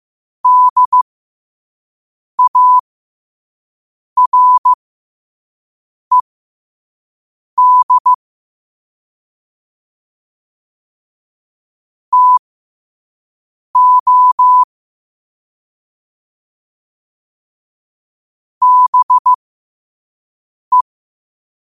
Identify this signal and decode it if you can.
Morse code, heard as a keyed tone, about 15 wpm, spaced as for 5 wpm, 1000 Hz. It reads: DARED TO BE
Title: 5 WPM morse code quotes for Fri, 01 Mar 2024